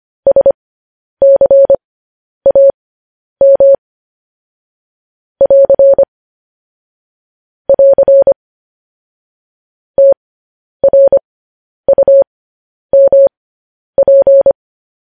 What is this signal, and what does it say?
Morse code, heard as a keyed tone, about 25 wpm, spaced as for 10 wpm, 550 Hz. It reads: SCAM <AR> <AR> TRUMP
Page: news-headlines-in-morse-code-at-10-wpm